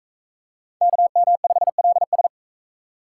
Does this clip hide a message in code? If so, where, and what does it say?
Yes: Morse code from 0.8 to 2.3 s, heard as a keyed tone, about 42 wpm, 700 Hz. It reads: KM5LS